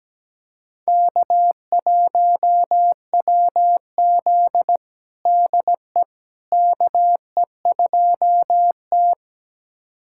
Morse code, heard as a keyed tone, about 17 wpm, 700 Hz. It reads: K1WZ DE KE2T
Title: USA Callsign DE Callsign - 4 Characters - 17wpm